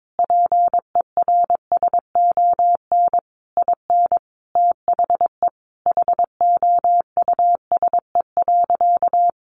Morse code, heard as a keyed tone, about 22 wpm, 700 Hz. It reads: PERSON IN THE HOUSE.
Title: Sentences from Top 1300 Words 22wpm